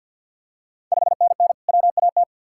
Morse code, heard as a keyed tone, about 50 wpm, 700 Hz. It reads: 5NN WAT
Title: Ontario QSO Party Exchange 50wpm